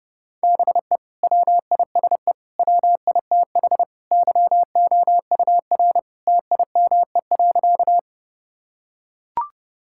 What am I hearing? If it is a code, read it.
Morse code, heard as a keyed tone, about 32 wpm, 700 Hz. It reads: BE WISE WITH YOUR TIME.
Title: Sentences from Top 1600 Words 30wpm